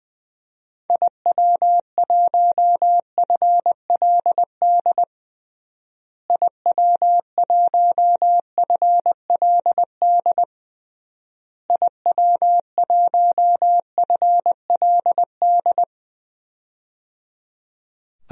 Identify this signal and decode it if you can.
Morse code, heard as a keyed tone, about 20 wpm, 700 Hz. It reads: IW1FLD IW1FLD IW1FLD
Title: DXCC-100 Call Signs - 6 Characters - Repeated 3x - 20wpm